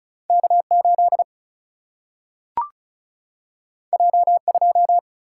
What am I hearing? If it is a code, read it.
Morse code, heard as a keyed tone, about 36 wpm, 700 Hz. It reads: K8 J2
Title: Single-Letter Single-Number 35wpm